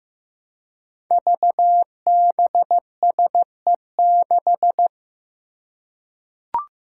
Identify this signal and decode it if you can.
Morse code, heard as a keyed tone, about 15 wpm, 700 Hz. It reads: VBSE6
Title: Five Challenging Letter-Number Combinations 15wpm